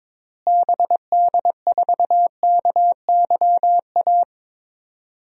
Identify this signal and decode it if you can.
Morse code, heard as a keyed tone, about 22 wpm, 700 Hz. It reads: BD4KYA
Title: DXCC-100 Call Signs - 6 Characters - Repeated 2x - 22wpm